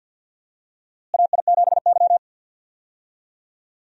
Morse code, heard as a keyed tone, about 48 wpm, 700 Hz. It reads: AI7Y
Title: USA Call Signs - 4 Characters - Repeated 3x - 50wpm